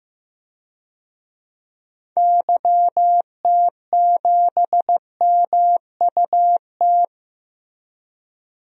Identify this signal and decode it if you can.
Morse code, heard as a keyed tone, about 16 wpm, 700 Hz. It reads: YT7MUT
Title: DXCC-100 Call Signs - 6 Characters - Repeated 3x - 15wpm